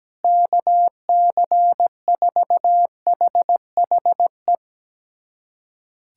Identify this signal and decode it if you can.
Morse code, heard as a keyed tone, about 17 wpm, 700 Hz. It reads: KC4HHE
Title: USA Callsign DE Callsign 17wpm